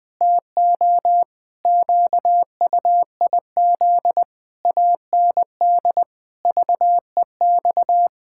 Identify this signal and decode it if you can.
Morse code, heard as a keyed tone, about 21 wpm, 700 Hz. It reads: TO QUIZ AND VEX
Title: Pangram Challenge v2 - 20wpm